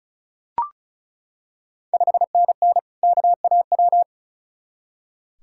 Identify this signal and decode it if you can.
Morse code, heard as a keyed tone, about 36 wpm, 690 Hz. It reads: 5NN KAW
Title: Ontario QSO Party Exchange 35wpm